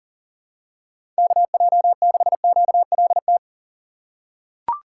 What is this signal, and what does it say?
Morse code, heard as a keyed tone, about 40 wpm, 700 Hz. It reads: KJ6QLT
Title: USA Call Signs - 6 Characters - 3x Character Spacing - 40wpm